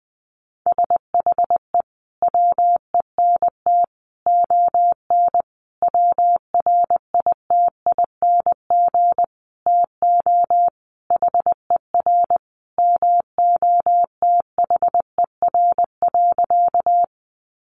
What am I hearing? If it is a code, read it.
Morse code, heard as a keyed tone, about 20 wpm, 700 Hz. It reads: SHE WENT ON WRITING TO HER MOTHER.
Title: Sentences from Top 800 Words 20wpm